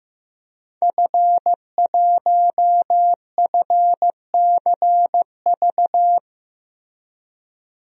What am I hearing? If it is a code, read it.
Morse code, heard as a keyed tone, about 15 wpm, 700 Hz. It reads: F1FCV
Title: DXCC-100 Call Signs - 5 Characters - 15wpm